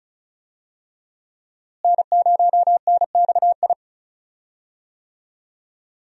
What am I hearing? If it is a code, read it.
Morse code, heard as a keyed tone, about 36 wpm, 700 Hz. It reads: N0NXI
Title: USA Call Signs - 5 Characters - Repeated 3x - 35wpm